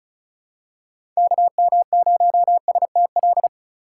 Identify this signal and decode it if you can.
Morse code, heard as a keyed tone, about 34 wpm, 700 Hz. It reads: KM0STL
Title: USA Call Signs - 6 Characters - 2x Character Spacing - 35wpm